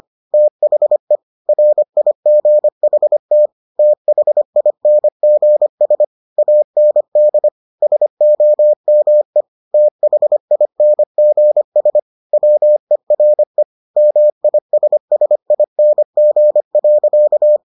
Morse code, heard as a keyed tone, about 25 wpm, 600 Hz. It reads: THE RIGHT THINGS AND SOME THINGS WERE MISSING.